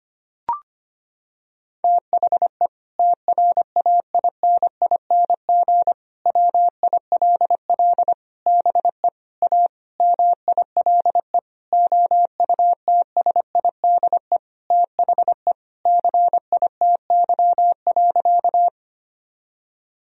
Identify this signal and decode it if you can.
Morse code, heard as a keyed tone, about 25 wpm, 700 Hz. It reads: THE TRAINING WILL BE A MILE OUTSIDE THE CITY.
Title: Sentences from Top 1000 Words - Review Part 4 - 25wpm